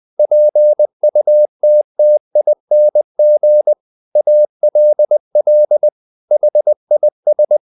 Morse code, heard as a keyed tone, about 20 wpm, 600 Hz. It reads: PUTTING ALL HIS
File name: the old man and the sea part 2 20wpm